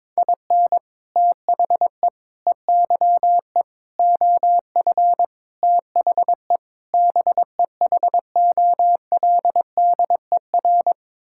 Morse code, heard as a keyed tone, about 21 wpm, 700 Hz. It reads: IN THE EYE OF THE BEHOLDER
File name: practice / 22wpm / Sets of 6 Words - 22wpm